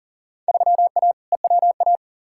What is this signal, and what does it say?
Morse code, heard as a keyed tone, about 40 wpm, 700 Hz. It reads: 3A EWA
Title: Field Day Exchanges Encore - 40wpm